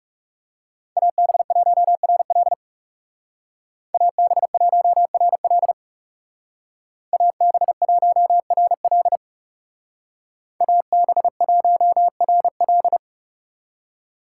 Morse code, heard as a keyed tone, about 35 wpm, 700 Hz. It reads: AB1RL AB1RL AB1RL AB1RL
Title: Speed Racing - USA Call Signs 30wpm